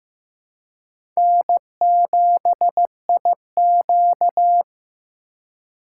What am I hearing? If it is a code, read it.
Morse code, heard as a keyed tone, about 15 wpm, 700 Hz. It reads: N7IQ
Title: USA Call Signs - 3x Character Spacing - 15wpm